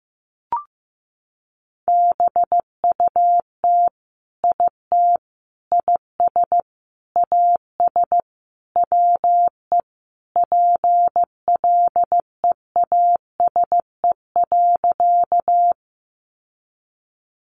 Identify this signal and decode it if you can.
Morse code, heard as a keyed tone, about 15 wpm, 700 Hz. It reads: BUT IT IS AS WE PLEASE.
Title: Sentences from Top 1500 Words 15wpm